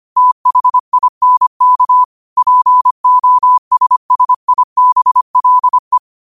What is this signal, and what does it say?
Morse code, heard as a keyed tone, about 25 wpm, 1000 Hz. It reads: THINK POSSIBLE
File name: qotdfiles / 25wpm